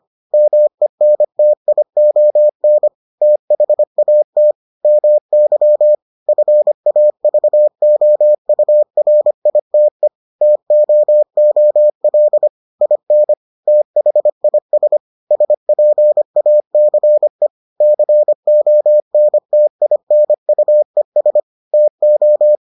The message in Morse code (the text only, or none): MENTION THAT MY FAVOURITE TOOL IN THIS SPACE CONTINUES TO